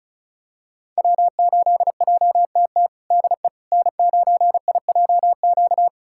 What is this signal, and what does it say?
Morse code, heard as a keyed tone, about 36 wpm, 700 Hz. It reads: W8JTT DE N9IJQ
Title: USA Callsign DE Callsign - 5 Characters - 35wpm